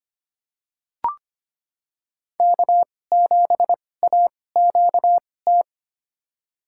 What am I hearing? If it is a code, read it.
Morse code, heard as a keyed tone, about 25 wpm, spaced as for 18 wpm, 700 Hz. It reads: K7AQT
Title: USA Call Signs - 5 Characters - 2x Character Spacing - 25wpm